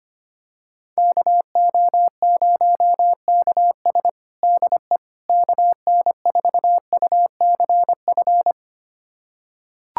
Morse code, heard as a keyed tone, about 25 wpm, 700 Hz. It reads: KO0KS DE KN4UCF
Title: USA Callsign DE Callsign 25wpm